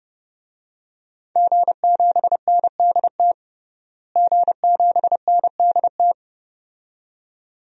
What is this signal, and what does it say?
Morse code, heard as a keyed tone, about 31 wpm, 700 Hz. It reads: G7NDT G7NDT
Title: DXCC-100 Call Signs - 5 Characters - Repeated 2x - 30wpm